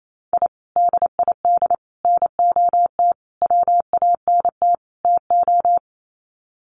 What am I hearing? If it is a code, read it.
Morse code, heard as a keyed tone, about 28 wpm, 700 Hz. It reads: I DID NOT WANT TO
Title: Sets of 5 Words Encore 28wpm